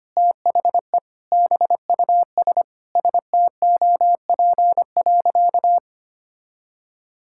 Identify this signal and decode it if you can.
Morse code, heard as a keyed tone, about 25 wpm, 700 Hz. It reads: THE BUS STOP.
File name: practice / 25wpm / Sentences with 2-Word Phrases - 25wpm